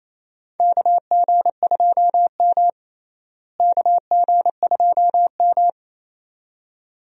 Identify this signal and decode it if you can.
Morse code, heard as a keyed tone, about 28 wpm, 700 Hz. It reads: KG2M KG2M
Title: USA Call Signs - 4 Characters - Repeated 2x - 28wpm